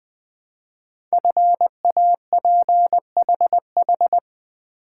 Morse code, heard as a keyed tone, about 20 wpm, 700 Hz. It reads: FAPHH
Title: Code Groups - 5 Letters - 20wpm